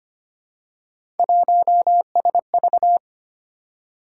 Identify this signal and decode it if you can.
Morse code, heard as a keyed tone, about 25 wpm, 700 Hz. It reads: 1SV
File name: Three Letter-Number Combinations - 25wpm